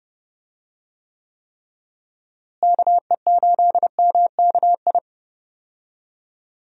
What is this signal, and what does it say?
Morse code, heard as a keyed tone, about 30 wpm, 700 Hz. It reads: KE8MKI
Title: USA Call Signs - 6 Characters - Repeated 3x - 30wpm